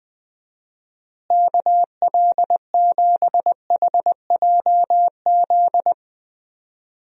Morse code, heard as a keyed tone, about 20 wpm, 700 Hz. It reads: KL7HJZ
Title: DXCC-100 Call Signs - 6 Characters - Repeated 2x - 20wpm